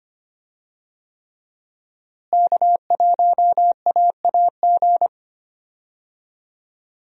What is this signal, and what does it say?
Morse code, heard as a keyed tone, about 25 wpm, 700 Hz. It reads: K1AAG
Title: USA Call Signs - 5 Characters - Repeated 3x - 25wpm